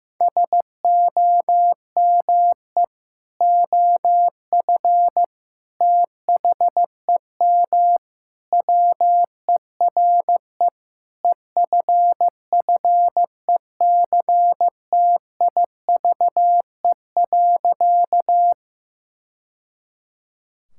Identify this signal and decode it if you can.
Morse code, heard as a keyed tone, about 15 wpm, 700 Hz. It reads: SOME OF THEM WERE EFFECTIVE.
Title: Sentences from Top 1000 Words - Review Part 6 - 15wpm